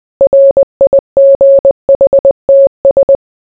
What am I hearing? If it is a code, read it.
Morse code, heard as a keyed tone, about 20 wpm, 550 Hz. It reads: RIGHTS